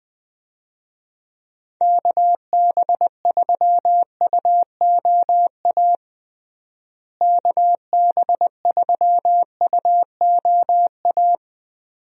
Morse code, heard as a keyed tone, about 21 wpm, 700 Hz. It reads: KB3UOA KB3UOA
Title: USA Call Signs - 6 Characters - Repeated 2x - 20wpm